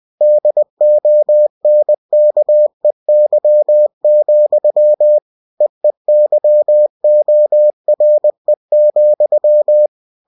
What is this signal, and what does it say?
Morse code, heard as a keyed tone, about 20 wpm, 600 Hz. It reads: DONKEY, EEYORE,
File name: cw-books / winnie-the-pooh / winnie-20wpm-page-0026